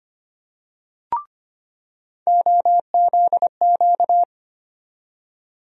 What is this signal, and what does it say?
Morse code, heard as a keyed tone, about 25 wpm, 700 Hz. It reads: OZQ
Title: USA Call Sign Postfixes - 3 Letters 25wpm